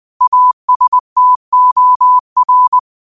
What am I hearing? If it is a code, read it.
Morse code, heard as a keyed tone, about 20 wpm, 1000 Hz. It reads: ASTOR